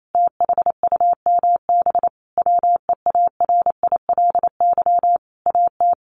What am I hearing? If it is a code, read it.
Morse code, heard as a keyed tone, about 28 wpm, 700 Hz. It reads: THUMB WEARILY AT
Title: A Sound Of Thunder 28wpm